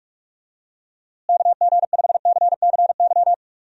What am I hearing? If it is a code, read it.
Morse code, heard as a keyed tone, about 46 wpm, 700 Hz. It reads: KG5CCY
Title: USA Call Signs - 6 Characters - Repeated 2x - 45wpm